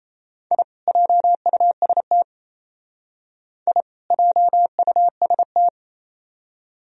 Morse code, heard as a keyed tone, about 34 wpm, 700 Hz. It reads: I JUST I JUST
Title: Speed-Racing - Sets of 2 Words 28wpm